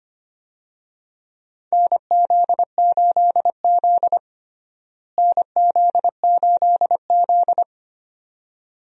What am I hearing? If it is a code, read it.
Morse code, heard as a keyed tone, about 25 wpm, 700 Hz. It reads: NZ8Z NZ8Z
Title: USA Call Signs - 4 Characters - Repeated 2x - 25wpm